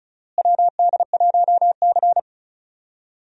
Morse code, heard as a keyed tone, about 34 wpm, 700 Hz. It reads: WD1C
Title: USA Call Signs - 4 Characters - Repeated 2x - 35wpm